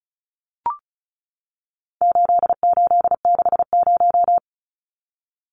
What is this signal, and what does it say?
Morse code, heard as a keyed tone, about 34 wpm, 700 Hz. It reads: 8860
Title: Numbers - 4-Digit 35wpm